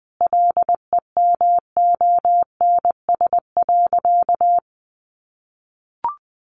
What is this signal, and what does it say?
Morse code, heard as a keyed tone, about 20 wpm, 700 Hz. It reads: LEMONS.
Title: Additional Easy Phrases 20wpm